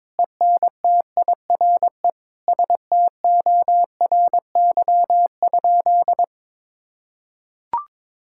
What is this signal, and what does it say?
Morse code, heard as a keyed tone, about 22 wpm, 700 Hz. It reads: ENTIRE STORY?
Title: Sentences from Top 1000 Words - Review Part 5 - 22wpm